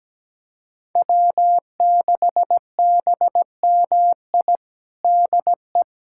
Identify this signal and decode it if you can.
Morse code, heard as a keyed tone, about 17 wpm, 700 Hz. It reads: W6BMI DE
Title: USA Callsign DE Callsign - 5 Characters - 17wpm